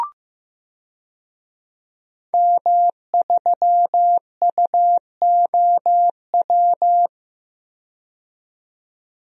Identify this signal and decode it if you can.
Morse code, heard as a keyed tone, about 15 wpm, 700 Hz. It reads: M3UOW
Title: DXCC-100 Call Signs - 5 Characters - 15wpm